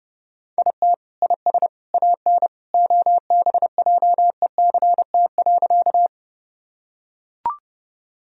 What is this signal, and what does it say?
Morse code, heard as a keyed tone, about 30 wpm, 700 Hz. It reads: IT IS AN OBJECT.
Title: Sentences from Top 1000 Words - Review Part 3 - 30wpm